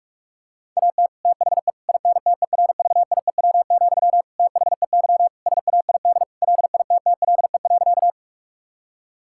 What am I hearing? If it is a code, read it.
Morse code, heard as a keyed tone, about 47 wpm, 700 Hz. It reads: AT THE INTERVIEW, THEY SAID LITTLE.
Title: Sentences from Top 1600 Words 45wpm